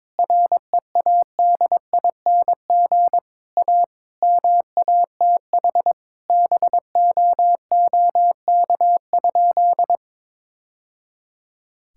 Morse code, heard as a keyed tone, about 22 wpm, 700 Hz. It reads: READING A MATH BOOK?
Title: Sentences from Top 1000 Words - Review Part 6 - 22wpm